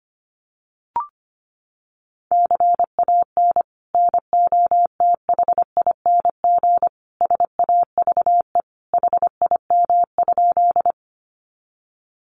Morse code, heard as a keyed tone, about 25 wpm, 700 Hz. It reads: CAN NOTHING SAVE HIM?
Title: Sentences from Top 1100 Words 25wpm